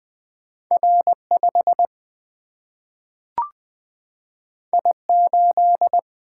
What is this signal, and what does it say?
Morse code, heard as a keyed tone, about 20 wpm, 700 Hz. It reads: R5 I8
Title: Single-Letter Single-Number 20wpm